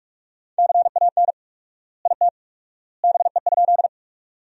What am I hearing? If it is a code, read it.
Morse code, heard as a keyed tone, about 46 wpm, 700 Hz. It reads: CAN IT BE?
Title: 4x Word-Spacing - Sentences from Top 100 Words - 45wpm